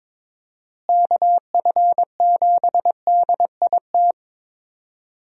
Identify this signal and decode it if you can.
Morse code, heard as a keyed tone, about 21 wpm, 700 Hz. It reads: KF7DIT
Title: USA Call Signs - 6 Characters - Repeated 3x - 22wpm